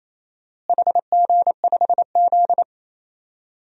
Morse code, heard as a keyed tone, about 28 wpm, 700 Hz. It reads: HG5Z
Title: DXCC-100 Call Signs - 4 Characters - Repeated 2x - 28wpm